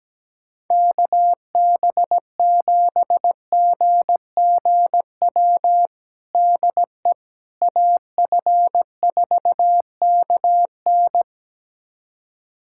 Morse code, heard as a keyed tone, about 17 wpm, 700 Hz. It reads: KB7GGW DE AF4KN
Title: USA Callsign DE Callsign 17wpm